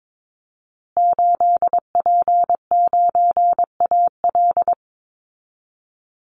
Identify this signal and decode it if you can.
Morse code, heard as a keyed tone, about 22 wpm, 700 Hz. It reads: 8P9AL
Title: Call Signs Repeated 3x - 22wpm